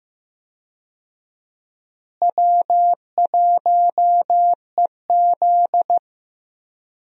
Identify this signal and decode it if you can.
Morse code, heard as a keyed tone, about 15 wpm, 700 Hz. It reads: W1EZ
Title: USA Call Signs - 4 Characters - Repeated 2x - 15wpm